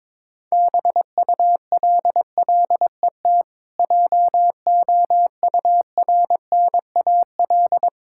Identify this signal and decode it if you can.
Morse code, heard as a keyed tone, about 22 wpm, 690 Hz. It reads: BULLET JOURNAL
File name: Difficult Words with Hints - 22wpm